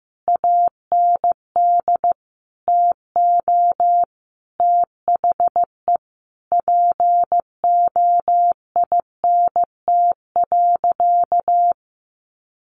Morse code, heard as a keyed tone, about 15 wpm, 700 Hz. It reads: AND TO THE POINT.
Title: Sentences from Top 1400 Words 15wpm